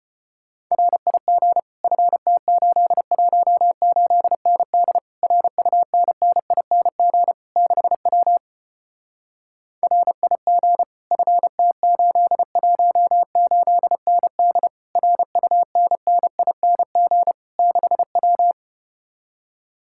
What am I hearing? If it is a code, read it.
Morse code, heard as a keyed tone, about 33 wpm, 700 Hz. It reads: RIG FT818ND RUNNING 6W RIG FT818ND RUNNING 6W